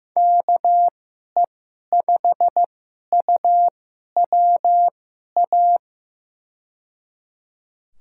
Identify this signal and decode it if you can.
Morse code, heard as a keyed tone, about 15 wpm, spaced as for 11 wpm, 700 Hz. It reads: KE5UWA
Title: USA Call Signs - 6 Characters - 2x Character Spacing - 15wpm